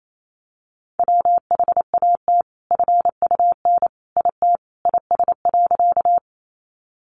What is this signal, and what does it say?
Morse code, heard as a keyed tone, about 28 wpm, 700 Hz. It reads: WHAT FUN IT IS.
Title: Sentences from Top 700 Words 28wpm